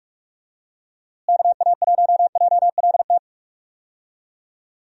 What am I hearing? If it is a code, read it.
Morse code, heard as a keyed tone, about 43 wpm, 700 Hz. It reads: KA1JLT
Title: USA Call Signs - 6 Characters - Repeated 2x - 45wpm